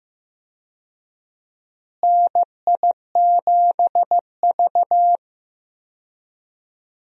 Morse code, heard as a keyed tone, about 15 wpm, 700 Hz. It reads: NI7V
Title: USA Call Signs - 4 Characters - Repeated 3x - 15wpm